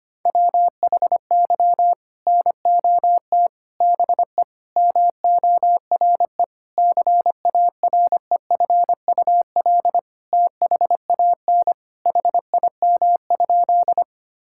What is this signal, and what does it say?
Morse code, heard as a keyed tone, about 25 wpm, 700 Hz. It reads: WHY NOT BE MORE CAREFUL THAN HIM?
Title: Sentences from Top 1600 Words 25wpm